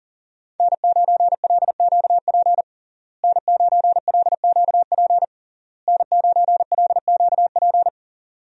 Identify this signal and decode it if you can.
Morse code, heard as a keyed tone, about 40 wpm, 700 Hz. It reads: N9LQP N9LQP N9LQP
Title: USA Call Signs - 5 Characters - Repeated 3x - 40wpm